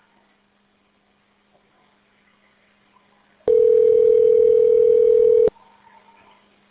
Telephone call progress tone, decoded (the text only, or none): ringback tone